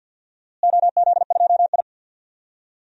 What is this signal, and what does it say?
Morse code, heard as a keyed tone, about 51 wpm, 700 Hz. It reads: OZ2I
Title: DXCC-100 Call Signs - 4 Characters - Repeated 2x - 50wpm